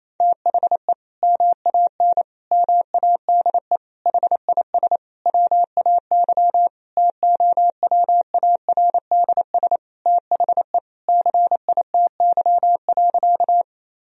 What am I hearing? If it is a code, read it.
Morse code, heard as a keyed tone, about 28 wpm, 700 Hz. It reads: THE MAN MADE HIS WAY TOWARDS THE CITY.
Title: Sentences from Top 1000 Words - Review Part 2 - 28wpm